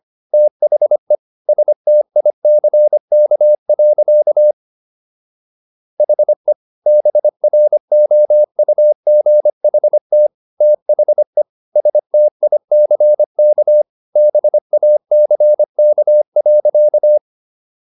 Morse code, heard as a keyed tone, about 25 wpm, 600 Hz. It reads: THE STICK. HE BROUGHT THE STICK BACK.